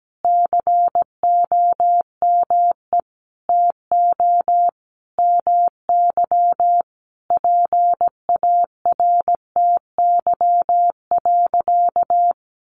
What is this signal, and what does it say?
Morse code, heard as a keyed tone, about 17 wpm, 700 Hz. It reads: COME TO MY PARTY.
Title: Sentences from Top 1400 Words 17wpm